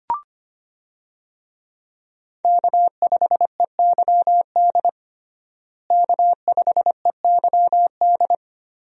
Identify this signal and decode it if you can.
Morse code, heard as a keyed tone, about 25 wpm, 700 Hz. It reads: K5EYD K5EYD
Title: USA Call Signs - 5 Characters - Repeated 2x - 25wpm